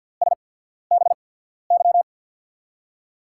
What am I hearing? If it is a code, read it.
Morse code, heard as a keyed tone, about 49 wpm, spaced as for 14 wpm, 700 Hz. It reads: SBY
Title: USA Call Signs - 5 Characters - 8x Character Spacing - 50wpm